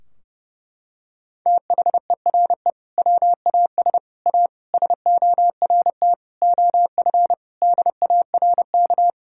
Morse code, heard as a keyed tone, about 30 wpm, 700 Hz. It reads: THERE WAS A SORT OF DARK
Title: Sentences from Top 1600 Words 30wpm